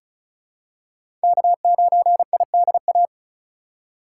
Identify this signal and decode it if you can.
Morse code, heard as a keyed tone, about 36 wpm, 700 Hz. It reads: K9IDA
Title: USA Call Signs - 5 Characters - 2x Character Spacing - 35wpm